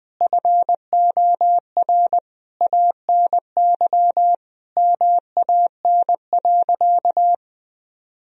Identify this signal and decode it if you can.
Morse code, heard as a keyed tone, about 20 wpm, 700 Hz. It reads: FOR ANY MAN.
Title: Sentences from Top 1000 Words - Review Part 3 - 20wpm